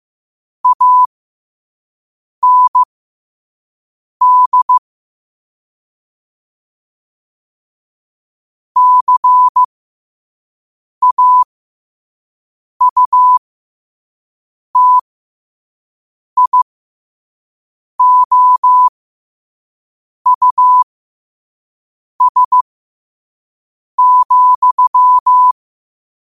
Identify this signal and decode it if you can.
Morse code, heard as a keyed tone, about 15 wpm, spaced as for 5 wpm, 1000 Hz. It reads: AND CAUTIOUS,
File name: qotdfiles / 5wpm